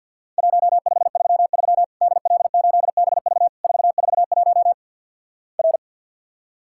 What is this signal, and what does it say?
Morse code, heard as a keyed tone, about 50 wpm, 700 Hz. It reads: 1533 DL8BV 441 R
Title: Worked All Europe DX Contest - QTC Exchange with Repeats - 50wpm